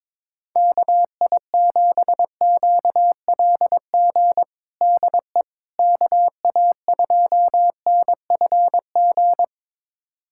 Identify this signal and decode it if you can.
Morse code, heard as a keyed tone, about 22 wpm, 700 Hz. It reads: KI7QLG DE KA2NFG
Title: USA Callsign DE Callsign - 6 Characters - 22wpm